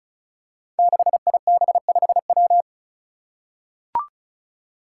Morse code, heard as a keyed tone, about 36 wpm, 700 Hz. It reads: 6IB5W
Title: Five Challenging Letter-Number Combinations 35wpm